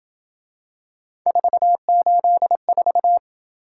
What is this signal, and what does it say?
Morse code, heard as a keyed tone, about 27 wpm, 700 Hz. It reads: 484